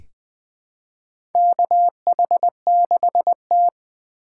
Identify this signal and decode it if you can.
Morse code, heard as a keyed tone, about 20 wpm, 700 Hz. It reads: KH6T
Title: USA Call Signs - 4 Characters - 4x Character Spacing - 20wpm